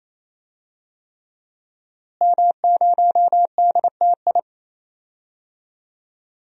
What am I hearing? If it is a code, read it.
Morse code, heard as a keyed tone, about 28 wpm, 700 Hz. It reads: M0DTI
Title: DXCC-100 Call Signs - 5 Characters - Repeated 3x - 28wpm